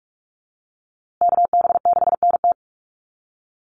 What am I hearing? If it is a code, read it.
Morse code, heard as a keyed tone, about 44 wpm, 700 Hz. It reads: KB6NT
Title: USA Call Signs Repeated 3x 45wpm